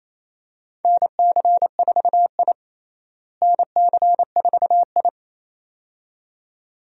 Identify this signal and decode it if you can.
Morse code, heard as a keyed tone, about 28 wpm, 700 Hz. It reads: NC4I NC4I
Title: USA Call Signs - 4 Characters - Repeated 2x - 28wpm